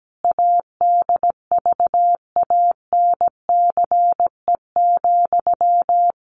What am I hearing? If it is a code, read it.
Morse code, heard as a keyed tone, about 17 wpm, 700 Hz. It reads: ADVANCE,
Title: Sentences from Top 1400 Words 17wpm